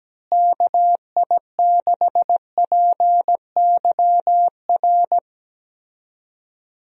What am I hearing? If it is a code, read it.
Morse code, heard as a keyed tone, about 17 wpm, 700 Hz. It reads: KI6PYR
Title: USA Call Signs - 6 Characters - Repeated 2x - 17wpm